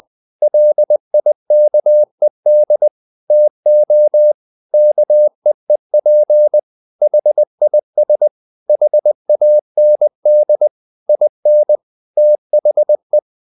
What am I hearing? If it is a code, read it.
Morse code, heard as a keyed tone, about 20 wpm, 600 Hz. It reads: LIKED TO KEEP HIS HAND IN THE
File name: the old man and the sea / the old man and the sea part 1 20wpm